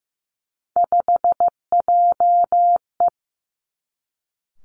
Morse code, heard as a keyed tone, about 15 wpm, 700 Hz. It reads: 5JE
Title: Three Challenging Letter-Number Combinations 15wpm